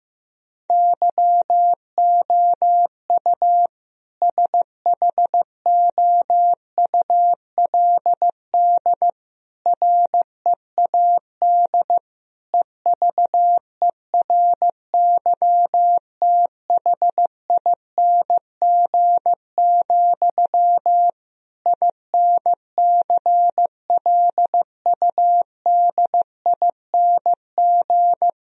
Morse code, heard as a keyed tone, about 15 wpm, 700 Hz. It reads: YOU SHOULD READ EVERYTHING, INCLUDING
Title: Sentences from Top 1000 Words - Review Part 4 - 15wpm